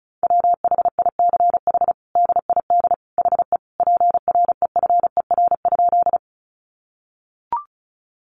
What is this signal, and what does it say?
Morse code, heard as a keyed tone, about 34 wpm, 700 Hz. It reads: WHICH DID HE PREFER?
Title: Sentences from Top 1100 Words 35wpm